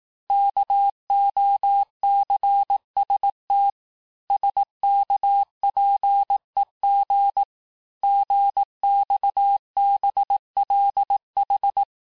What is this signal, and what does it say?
Morse code, heard as a keyed tone, about 18 wpm, 800 Hz. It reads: KOCST SKPEG GXBLH
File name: cours-du-lundi files / cours-1852-texte-chiffre-vitesse-17MPM